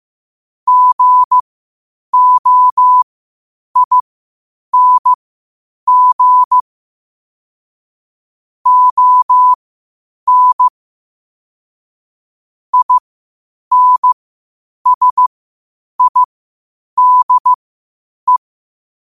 Morse code, heard as a keyed tone, about 15 wpm, spaced as for 8 wpm, 1000 Hz. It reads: GOING ON INSIDE
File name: qotdfiles / 8wpm